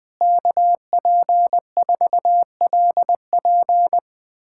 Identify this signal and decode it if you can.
Morse code, heard as a keyed tone, about 21 wpm, 700 Hz. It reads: KP4LP
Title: USA Callsign DE Callsign 20wpm